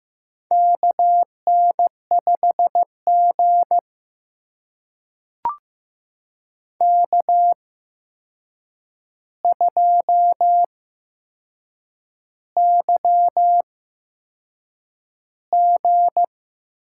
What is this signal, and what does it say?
Morse code, heard as a keyed tone, about 15 wpm, spaced as for 6 wpm, 700 Hz. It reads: KN5G K2YG
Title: USA Call Signs - 4 Characters - 8x Character Spacing - 15wpm